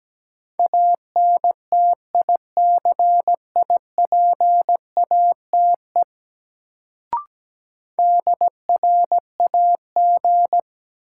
Morse code, heard as a keyed tone, about 17 wpm, 700 Hz. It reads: ANTICIPATE DRAG
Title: Sentences from Top 1500 Words - New Word Drill - 17wpm